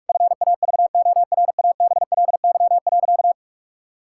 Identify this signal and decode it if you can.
Morse code, heard as a keyed tone, about 44 wpm, 700 Hz. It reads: FAVORABLY.
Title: Pangram Challenge v2 - 45wpm